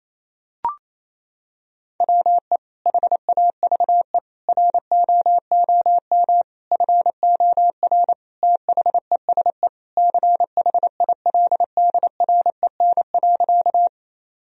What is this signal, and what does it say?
Morse code, heard as a keyed tone, about 28 wpm, 700 Hz. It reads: WE HAVE ROOM FOR THESE CHILDREN.
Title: Sentences from Top 1000 Words - Review Part 1 - 28wpm